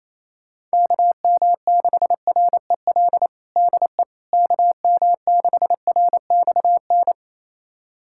Morse code, heard as a keyed tone, about 28 wpm, 700 Hz. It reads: KM6REL DE KM6RXN
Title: USA Callsign DE Callsign - 6 Characters - 28wpm